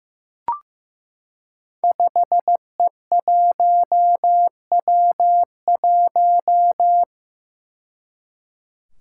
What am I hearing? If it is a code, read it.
Morse code, heard as a keyed tone, about 15 wpm, 700 Hz. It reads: 5E1W1
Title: Five Challenging Letter-Number Combinations 15wpm